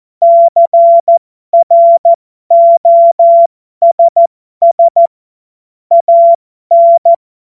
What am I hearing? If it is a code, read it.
Morse code, heard as a keyed tone, about 14 wpm, spaced as for 12 wpm, 680 Hz. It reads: CROSS AN
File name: Morsekurs / Witze